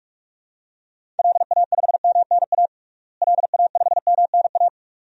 Morse code, heard as a keyed tone, about 46 wpm, 700 Hz. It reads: LA5MNA LA5MNA
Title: DXCC-100 Call Signs - 6 Characters - Repeated 2x - 45wpm